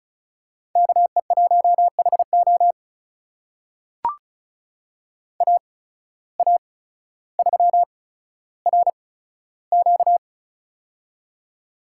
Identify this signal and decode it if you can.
Morse code, heard as a keyed tone, about 36 wpm, spaced as for 15 wpm, 700 Hz. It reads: KE1HO AA3RQ
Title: USA Call Signs - 5 Characters - 8x Character Spacing - 35wpm